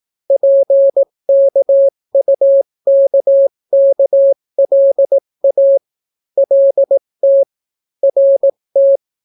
Morse code, heard as a keyed tone, about 18 wpm, 550 Hz. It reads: PKUKKLA LT RT